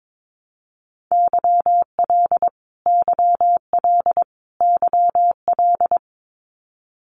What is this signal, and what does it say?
Morse code, heard as a keyed tone, about 22 wpm, 700 Hz. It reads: YL YL YL
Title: Abbreviations Repeated 3x - 22wpm